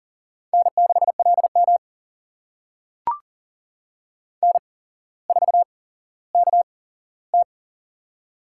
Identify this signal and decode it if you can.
Morse code, heard as a keyed tone, about 40 wpm, spaced as for 17 wpm, 700 Hz. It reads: N6LM N4KT
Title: USA Call Signs - 4 Characters - 8x Character Spacing - 40wpm